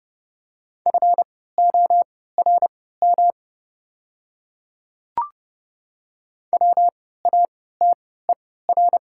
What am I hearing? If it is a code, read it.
Morse code, heard as a keyed tone, about 30 wpm, spaced as for 17 wpm, 690 Hz. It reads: FORM WATER
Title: 3x Character Spacing - Top 500 Words - 30wpm